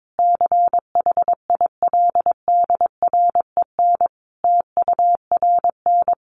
Morse code, heard as a keyed tone, about 22 wpm, 700 Hz. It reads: CHILDREN TURN
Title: Sentences from Top 600 Words 22wpm